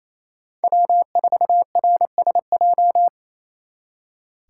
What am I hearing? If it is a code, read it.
Morse code, heard as a keyed tone, about 36 wpm, 700 Hz. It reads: W4RSJ